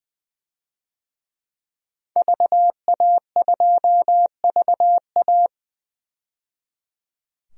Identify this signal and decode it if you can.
Morse code, heard as a keyed tone, about 20 wpm, 700 Hz. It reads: VA2VA
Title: DXCC-100 Call Signs - 5 Characters - 20wpm